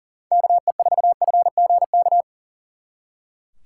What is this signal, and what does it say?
Morse code, heard as a keyed tone, about 49 wpm, 700 Hz. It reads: KE4FGK
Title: Speed Racing - USA 2x3 Call Signs 40wpm